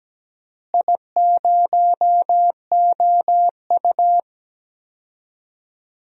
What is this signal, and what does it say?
Morse code, heard as a keyed tone, about 17 wpm, 700 Hz. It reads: I0OU
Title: DXCC-100 Call Signs - 4 Characters - Repeated 2x - 17wpm